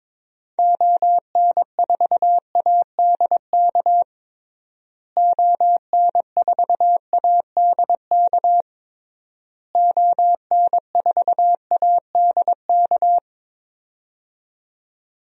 Morse code, heard as a keyed tone, about 22 wpm, 700 Hz. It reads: ON4ADK ON4ADK ON4ADK
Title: DXCC-100 Call Signs - 6 Characters - Repeated 3x - 22wpm